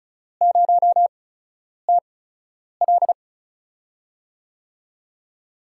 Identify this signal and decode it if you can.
Morse code, heard as a keyed tone, about 36 wpm, spaced as for 10 wpm, 700 Hz. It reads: 0TL
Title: USA Call Signs - 8x Character Spacing - 35wpm